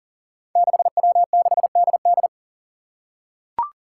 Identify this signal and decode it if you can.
Morse code, heard as a keyed tone, about 40 wpm, 700 Hz. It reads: 6W6DD
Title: Five Challenging Letter-Number Combinations 40wpm